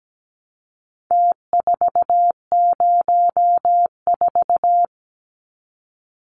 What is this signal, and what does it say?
Morse code, heard as a keyed tone, about 17 wpm, 700 Hz. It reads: T404
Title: Numbers - Contest Style - 4-Digit 17wpm